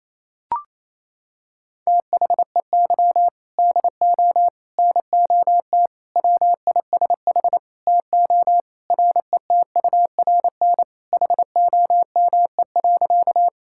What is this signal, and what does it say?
Morse code, heard as a keyed tone, about 28 wpm, 700 Hz. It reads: THEY DO NOT WISH TO RETURN HOME.
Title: Sentences from Top 1600 Words 28wpm